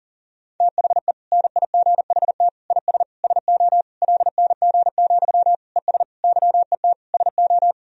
Morse code, heard as a keyed tone, about 37 wpm, 700 Hz. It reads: THE NIGHT IS SO LNG, ES YET SO
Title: Sentences from Top 500 Words and Abbreviations 40wpm